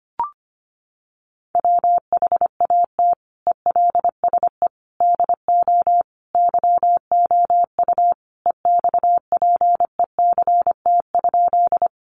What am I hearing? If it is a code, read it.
Morse code, heard as a keyed tone, about 25 wpm, 700 Hz. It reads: WHAT ELSE DO YOU EXPECT?
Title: Sentences from Top 1100 Words 25wpm